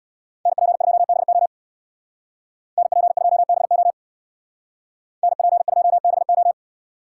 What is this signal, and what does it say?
Morse code, heard as a keyed tone, about 65 wpm, 700 Hz. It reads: NP2BO NP2BO NP2BO